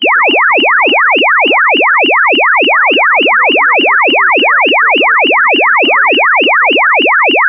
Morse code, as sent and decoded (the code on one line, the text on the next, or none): none